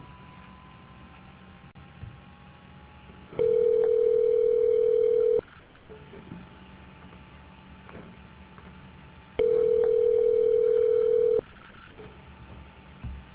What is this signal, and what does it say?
Ringback tone (North America), 440 + 480 Hz.